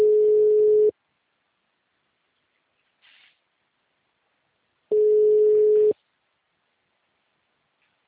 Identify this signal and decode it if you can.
Ringback tone (Europe, 425 Hz).
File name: out- -2015- - - .6573